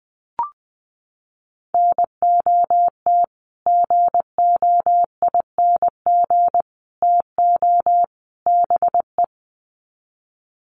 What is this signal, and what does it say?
Morse code, heard as a keyed tone, about 20 wpm, 700 Hz. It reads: NOT GOING TO BE
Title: Sets of 4 Words 20wpm